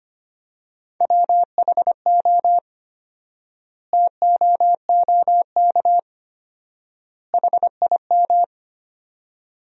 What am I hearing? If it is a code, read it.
Morse code, heard as a keyed tone, about 25 wpm, 700 Hz. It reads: WHO TOOK HIM
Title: 4x Word-Spacing - Sentences from Top 300 Words - 25wpm